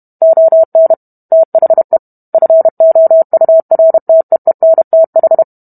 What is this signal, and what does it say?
Morse code, heard as a keyed tone, about 32 wpm, 650 Hz. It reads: ON THE FOURTEENTH